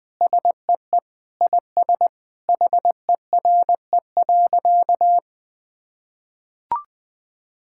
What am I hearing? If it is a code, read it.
Morse code, heard as a keyed tone, about 20 wpm, 700 Hz. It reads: SEE IS HERE.
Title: Sentences with 4-Word Phrases - 20wpm